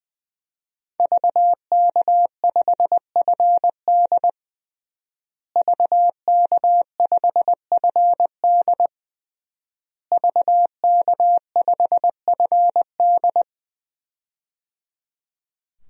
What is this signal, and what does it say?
Morse code, heard as a keyed tone, about 20 wpm, 700 Hz. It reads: VK5FD VK5FD VK5FD
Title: DXCC-100 Call Signs - 5 Characters - Repeated 3x - 20wpm